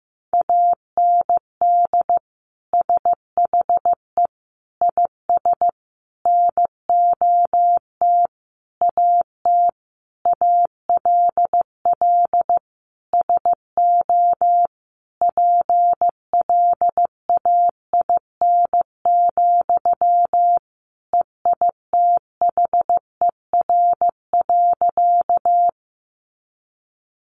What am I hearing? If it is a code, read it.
Morse code, heard as a keyed tone, about 15 wpm, 700 Hz. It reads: AND SHE IS NOT AT ALL SO PLAIN, EITHER.
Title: Sentences from Top 600 Words 15wpm